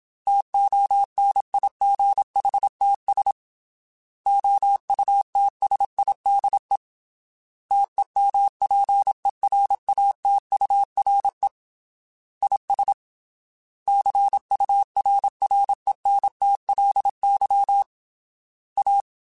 Morse code, heard as a keyed tone, about 27 wpm, 800 Hz. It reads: TONIGHTS OUTSIDE TEMPERATURE IS CURRENTLY A
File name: Weather Sets / Weather Sets-mp3s / Weather 4 28 @25